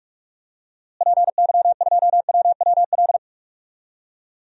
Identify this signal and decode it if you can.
Morse code, heard as a keyed tone, about 45 wpm, 700 Hz. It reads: PY2WWL
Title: DXCC-100 Call Signs - 6 Characters - Repeated 2x - 45wpm